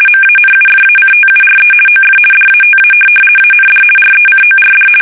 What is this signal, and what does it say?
RTTY, about 50 baud. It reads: MANUFACTURE OF DATA AND RECORD
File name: fsk850-50B / teletype-corp